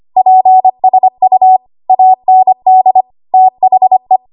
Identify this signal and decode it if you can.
Morse code, heard as a keyed tone, about 25 wpm, 750 Hz. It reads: PSU AND THE
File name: Morse / 25WPM